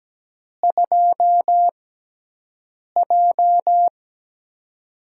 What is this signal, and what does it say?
Morse code, heard as a keyed tone, about 17 wpm, spaced as for 6 wpm, 700 Hz. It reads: 2J
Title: USA Call Signs - 4 Characters - 6x Character Spacing - 17wpm